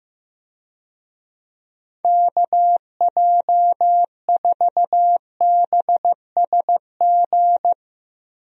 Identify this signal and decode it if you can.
Morse code, heard as a keyed tone, about 15 wpm, 700 Hz. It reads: KJ4BSG
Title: USA Call Signs - 6 Characters - 15wpm